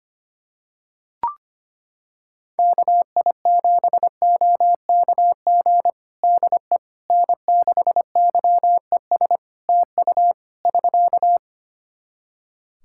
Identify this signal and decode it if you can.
Morse code, heard as a keyed tone, about 25 wpm, 700 Hz. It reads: KI7OKG DE N6YES TU <SK>
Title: Practice QSO - Exchange 4 - Ending - 25wpm